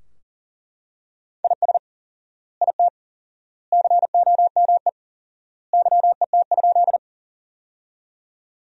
Morse code, heard as a keyed tone, about 37 wpm, 700 Hz. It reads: IS IT COME YET?
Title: 4x Word-Spacing - Sentences from Top 500 Words - 40wpm